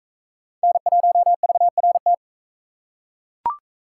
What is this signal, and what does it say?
Morse code, heard as a keyed tone, about 40 wpm, 700 Hz. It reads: N1VRT